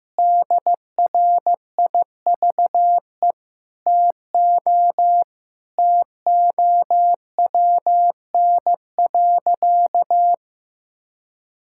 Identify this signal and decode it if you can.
Morse code, heard as a keyed tone, about 15 wpm, 700 Hz. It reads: DRIVE TO TOWN.
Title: Sentences from Top 1000 Words - Review Part 6 - 15wpm